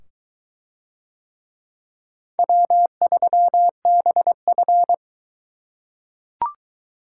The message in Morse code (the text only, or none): W3BF